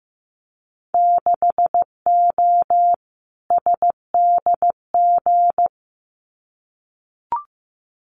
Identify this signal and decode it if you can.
Morse code, heard as a keyed tone, about 15 wpm, 700 Hz. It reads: 6O SDG
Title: Winter Field Day Exchanges Encore 15wpm